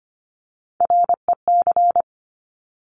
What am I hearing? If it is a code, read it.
Morse code, heard as a keyed tone, about 25 wpm, 700 Hz. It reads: REC
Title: Top Three Letter Combinations in Words 25wpm